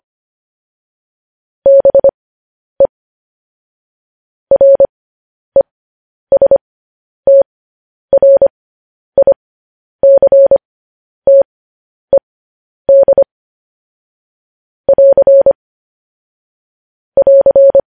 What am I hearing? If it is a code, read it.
Morse code, heard as a keyed tone, about 25 wpm, spaced as for 10 wpm, 550 Hz. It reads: BE RESTRICTED <AR> <AR>
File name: podcast / News headlines-1x-25x10